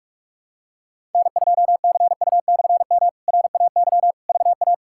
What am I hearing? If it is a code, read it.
Morse code, heard as a keyed tone, about 46 wpm, 700 Hz. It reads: N2CU/M RAY VA
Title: K1USN SST QSO Practice - 45wpm